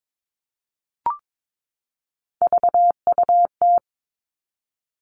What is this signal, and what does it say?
Morse code, heard as a keyed tone, about 22 wpm, 700 Hz. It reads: VUT
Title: Three Challenging Letter-Number Combinations 22wpm